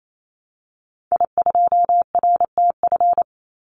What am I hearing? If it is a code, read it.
Morse code, heard as a keyed tone, about 28 wpm, 700 Hz. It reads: I2RTF
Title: Call Signs - 28wpm